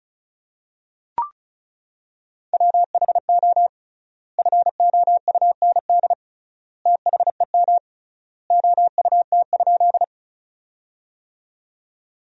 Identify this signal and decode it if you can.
Morse code, heard as a keyed tone, about 36 wpm, 700 Hz. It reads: WHO FOUND THEM OUT?
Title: 3x Word-Spacing - Sentences from Top 300 Words - 35wpm